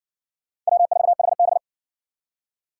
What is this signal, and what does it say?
Morse code, heard as a keyed tone, about 56 wpm, 700 Hz. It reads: W4HB